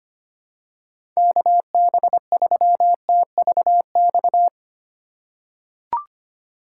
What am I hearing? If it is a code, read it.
Morse code, heard as a keyed tone, about 25 wpm, 700 Hz. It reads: KB3TVX
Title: USA Call Signs - 6 Characters - 3x Character Spacing - 25wpm